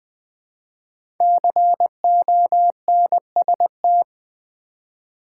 Morse code, heard as a keyed tone, about 20 wpm, 700 Hz. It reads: CONST
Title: Build A Word - 5 Letters At A Time - 12-Letter Words - 20wpm